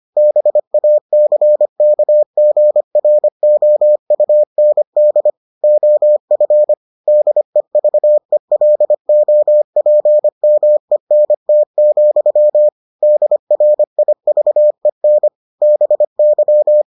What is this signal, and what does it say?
Morse code, heard as a keyed tone, about 25 wpm, 600 Hz. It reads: BACKGROUND OF DEVELOPMENT, DRIVEN BY